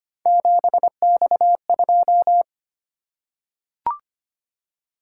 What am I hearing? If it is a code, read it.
Morse code, heard as a keyed tone, about 25 wpm, 700 Hz. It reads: 7X2
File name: Three Letter-Number Combinations - 25wpm